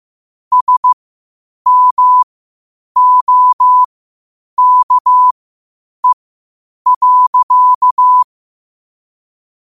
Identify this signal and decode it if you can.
Morse code, heard as a keyed tone, about 15 wpm, spaced as for 8 wpm, 1000 Hz. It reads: SMOKE.
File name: qotdfiles / 8wpm